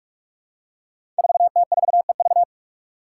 Morse code, heard as a keyed tone, about 46 wpm, 700 Hz. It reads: 4T4EV
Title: Five Challenging Letter-Number Combinations 45wpm